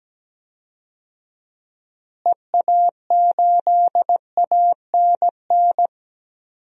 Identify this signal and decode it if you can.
Morse code, heard as a keyed tone, about 17 wpm, 700 Hz. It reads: EA8ANN
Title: DXCC-100 Call Signs - 6 Characters - Repeated 2x - 17wpm